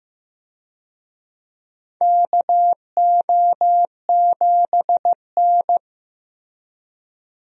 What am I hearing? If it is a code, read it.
Morse code, heard as a keyed tone, about 15 wpm, 700 Hz. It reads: KO7N
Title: USA Call Signs - 4 Characters - Repeated 3x - 15wpm